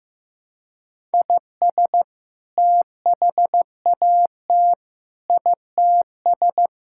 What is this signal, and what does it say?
Morse code, heard as a keyed tone, about 15 wpm, 700 Hz. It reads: IS THAT ITS
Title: Sentences from Top 1000 Words - Review Part 5 - 15wpm